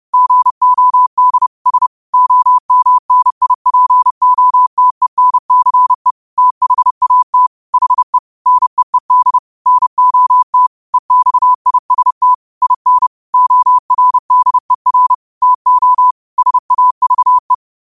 Morse code, heard as a keyed tone, about 30 wpm, 1000 Hz. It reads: GOD S OMNIPOTENCE THAT HE NEED NOT EXIST IN ORDER TO SAVE
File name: 30wpm